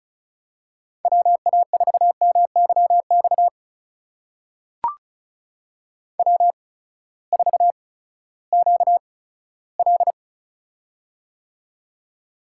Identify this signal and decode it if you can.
Morse code, heard as a keyed tone, about 36 wpm, spaced as for 18 wpm, 700 Hz. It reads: WA4MYX W4QL
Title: USA Call Signs - 8x Character Spacing - 35wpm